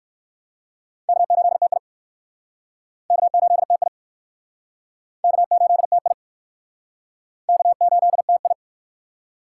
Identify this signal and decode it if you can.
Morse code, heard as a keyed tone, about 57 wpm, 700 Hz. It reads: K8TI K8TI K8TI K8TI